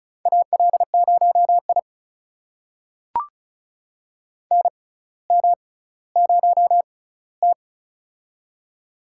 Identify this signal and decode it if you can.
Morse code, heard as a keyed tone, about 36 wpm, spaced as for 18 wpm, 700 Hz. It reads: AL0I NM0T
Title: USA Call Signs - 4 Characters - 6x Character Spacing - 35wpm